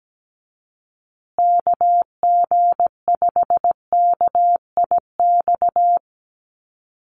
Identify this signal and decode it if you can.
Morse code, heard as a keyed tone, about 17 wpm, 700 Hz. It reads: KG5KIX
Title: USA Call Signs Repeated 2x 17wpm